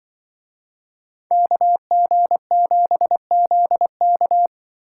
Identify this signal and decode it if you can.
Morse code, heard as a keyed tone, about 24 wpm, 700 Hz. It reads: KG7ZK
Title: Speed Racing - USA 2x2 Call Signs 20wpm